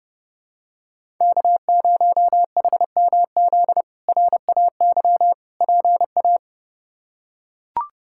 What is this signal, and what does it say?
Morse code, heard as a keyed tone, about 30 wpm, 700 Hz. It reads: K0HMZ RAY PA
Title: K1USN SST QSO Practice - 30wpm